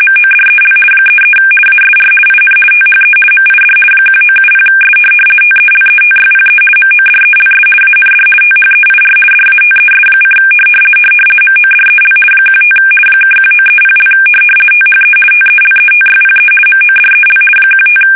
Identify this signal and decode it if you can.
RTTY, about 50 baud. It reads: MODEL 33 ASR IN THE MID 1970S TELETYPE MODELS AND THEIR DATES: MODEL 15 (1930) THE TELETYPE MODEL 15
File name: fsk850-50B / teletype-corp